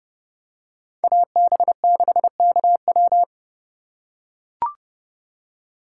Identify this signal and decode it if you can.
Morse code, heard as a keyed tone, about 28 wpm, 700 Hz. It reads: AB6KW